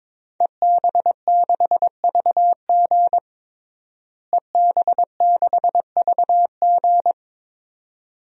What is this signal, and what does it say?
Morse code, heard as a keyed tone, about 22 wpm, 700 Hz. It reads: EB6VG EB6VG
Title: DXCC-100 Call Signs - 5 Characters - Repeated 2x - 22wpm